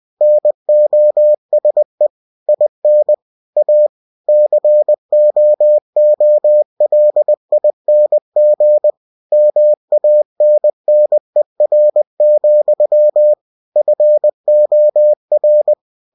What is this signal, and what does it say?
Morse code, heard as a keyed tone, about 20 wpm, 600 Hz. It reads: NOSE IN A COOLING MANNER, FOR
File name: cw-books / winnie-the-pooh / winnie-20wpm-page-0023